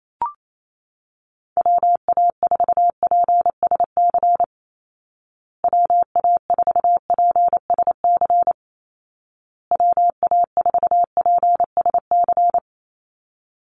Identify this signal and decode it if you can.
Morse code, heard as a keyed tone, about 28 wpm, 700 Hz. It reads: WA4PSC WA4PSC WA4PSC
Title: Call Signs Repeated 2x - 28wpm